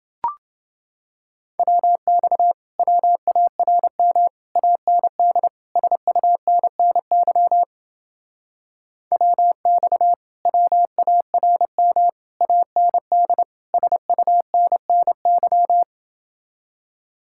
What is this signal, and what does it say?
Morse code, heard as a keyed tone, about 28 wpm, 700 Hz. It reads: WX WARM AND SUNNY WX WARM AND SUNNY